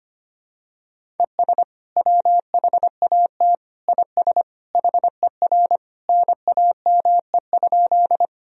Morse code, heard as a keyed tone, about 25 wpm, 700 Hz. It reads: ES WHAT IS HER NAME?
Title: Sentences from Top 200 Words and Abbreviations 25wpm